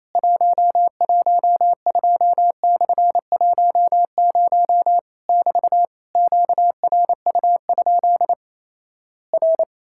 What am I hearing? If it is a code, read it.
Morse code, heard as a keyed tone, about 27 wpm, 700 Hz. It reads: 112/10 <BT> QRU? R
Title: Worked All Europe DX Contest - QTC Exchange - 28wpm